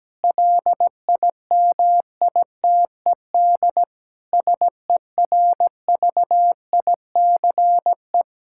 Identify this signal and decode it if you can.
Morse code, heard as a keyed tone, about 17 wpm, 700 Hz. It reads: LIMITED SERVICE
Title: Sentences from Top 1000 Words - Review Part 6 - 17wpm